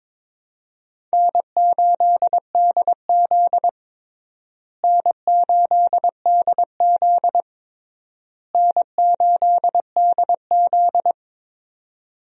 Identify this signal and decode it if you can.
Morse code, heard as a keyed tone, about 22 wpm, 700 Hz. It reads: N8DZ N8DZ N8DZ
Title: USA Call Signs - 4 Characters - Repeated 3x - 22wpm